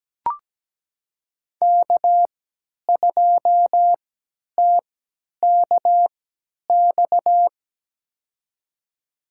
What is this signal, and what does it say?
Morse code, heard as a keyed tone, about 17 wpm, spaced as for 10 wpm, 700 Hz. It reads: K2TKX
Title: USA Call Signs - 5 Characters - 3x Character Spacing - 17wpm